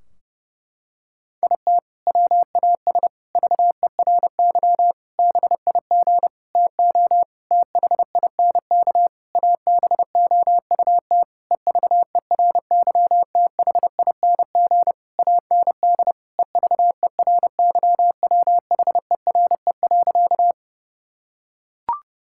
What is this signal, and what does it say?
Morse code, heard as a keyed tone, about 30 wpm, 700 Hz. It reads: IT WAS VERY BIG TO THINK ABOUT EVERYTHING AND EVERYWHERE.
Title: Sentences from Top 1600 Words 30wpm